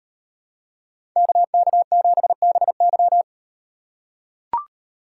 Morse code, heard as a keyed tone, about 38 wpm, 700 Hz. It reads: KK7BY